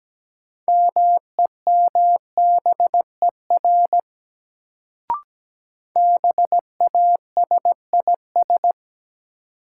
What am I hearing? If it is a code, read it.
Morse code, heard as a keyed tone, about 17 wpm, 700 Hz. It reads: MEMBER BASIS
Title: Sets of 3 Words - New Word Drill - 17wpm